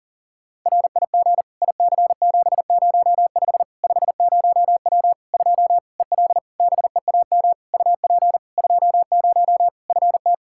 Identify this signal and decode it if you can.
Morse code, heard as a keyed tone, about 40 wpm, 700 Hz. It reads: RIG IC705 50W 2 EL BEAM UP 20 FT
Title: Practice QSO - Exchange 2 - Info - 40wpm